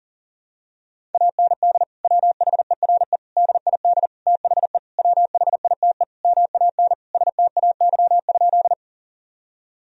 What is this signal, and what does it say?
Morse code, heard as a keyed tone, about 40 wpm, 700 Hz. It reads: AND WHERE DID THE WHITE MAN STAY?
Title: Sentences from Top 1000 Words - Review Part 1 - 40wpm